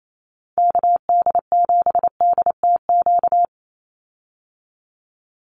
Morse code, heard as a keyed tone, about 28 wpm, 700 Hz. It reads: KD7DTQ
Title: USA Call Signs 28wpm